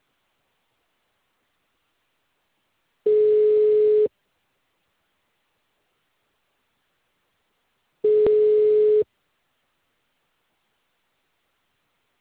Ringback tone (Europe, 425 Hz).